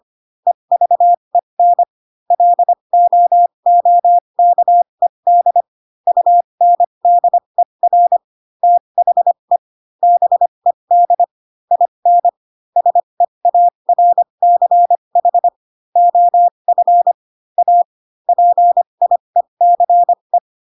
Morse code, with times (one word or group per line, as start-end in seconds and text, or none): 0.5-1.8 EVEN
2.3-5.6 LOOKED
6.1-8.2 UNDER
8.6-9.6 THE
10.0-11.2 BED
11.7-12.3 IN
12.8-15.5 SEARCH
16.0-17.1 OF
17.6-17.8 A
18.3-20.4 PIECE